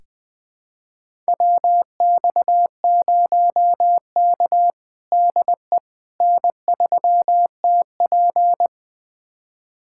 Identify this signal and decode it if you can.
Morse code, heard as a keyed tone, about 20 wpm, 700 Hz. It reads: WX0K DE N3TP
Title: USA Callsign DE Callsign - 4 Characters - 20wpm